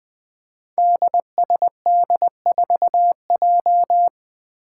Morse code, heard as a keyed tone, about 20 wpm, 700 Hz. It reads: DSD4J
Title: Five Challenging Letter-Number Combinations 20wpm